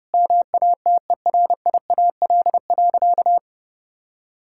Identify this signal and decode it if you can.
Morse code, heard as a keyed tone, about 32 wpm, 700 Hz. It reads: MATERIAL.
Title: Sentences from Top 1000 Words - Review Part 5 - 30wpm